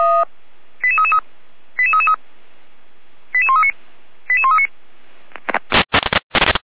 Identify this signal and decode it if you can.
Five-tone selective call (ZVEI), heard as quick runs of five tones, tones of about 80 ms. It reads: 88333 88333 88139 88139